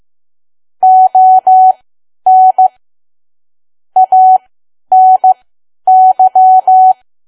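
Morse code, heard as a keyed tone, about 15 wpm, spaced as for 10 wpm, 750 Hz. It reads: ON ANY
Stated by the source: Morse code at 10 WPM for learning Morse code.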